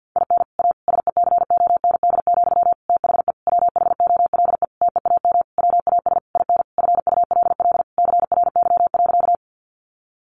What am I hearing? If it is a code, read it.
Morse code, heard as a keyed tone, about 48 wpm, 700 Hz. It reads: IN A SECOND, THE WHOLE TEAM WAS IN FULL CRY.
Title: Sentences from Top 1500 Words 50wpm